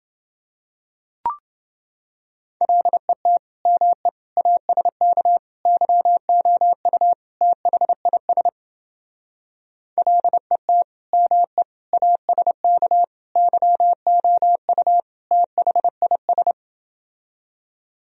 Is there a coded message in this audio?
Yes: Morse code from 2.6 to 16.5 s, heard as a keyed tone, about 29 wpm, 700 Hz. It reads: LET ME ASK YOU THIS LET ME ASK YOU THIS